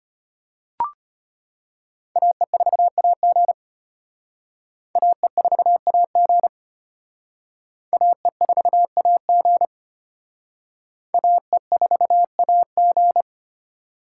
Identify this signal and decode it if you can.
Morse code, heard as a keyed tone, about 31 wpm, 700 Hz. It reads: AE4AG AE4AG AE4AG AE4AG
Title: Speed Racing - USA Call Signs 25wpm